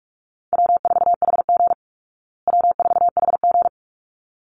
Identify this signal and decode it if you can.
Morse code, heard as a keyed tone, about 44 wpm, 700 Hz. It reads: W4HG W4HG
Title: Call Signs Repeated 3x - 45wpm